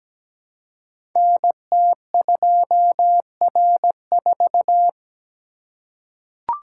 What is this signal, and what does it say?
Morse code, heard as a keyed tone, about 17 wpm, 700 Hz. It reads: NT2R4
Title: Code Groups - 5 Letter:Numbers - 17wpm